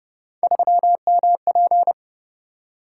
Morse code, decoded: 3MP

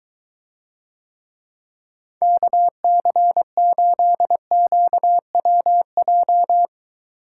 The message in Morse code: KC8QWJ

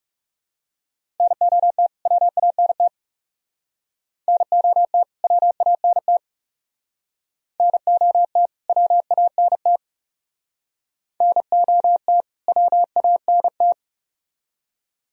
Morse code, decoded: NOT WANT NOT WANT NOT WANT NOT WANT

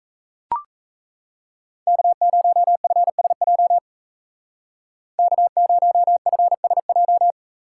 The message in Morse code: K0FSJ K0FSJ

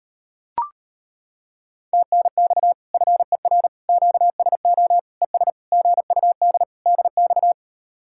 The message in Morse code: TNX FER QSO ES GUD DX

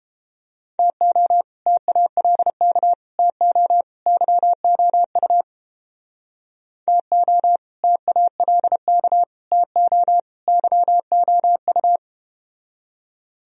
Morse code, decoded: TO TALK TO YOU TO TALK TO YOU

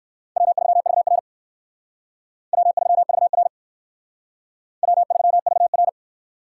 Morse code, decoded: W3VR W3VR W3VR

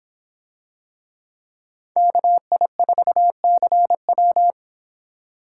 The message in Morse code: KI4CW

